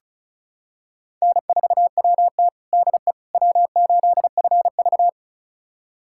Morse code, heard as a keyed tone, about 36 wpm, 700 Hz. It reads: N4WT DE W8FV
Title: USA Callsign DE Callsign - 4 Characters - 35wpm